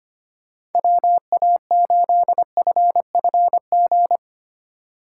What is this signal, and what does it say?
Morse code, heard as a keyed tone, about 32 wpm, 700 Hz. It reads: WA8FFG